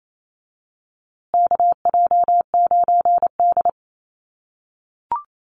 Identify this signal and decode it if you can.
Morse code, heard as a keyed tone, about 27 wpm, 700 Hz. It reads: KJ9D
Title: USA Call Signs Repeated 3x 28wpm